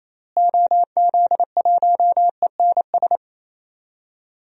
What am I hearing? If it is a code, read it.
Morse code, heard as a keyed tone, about 28 wpm, 700 Hz. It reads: OZ1ENS
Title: DXCC-100 Call Signs - 6 Characters - Repeated 2x - 28wpm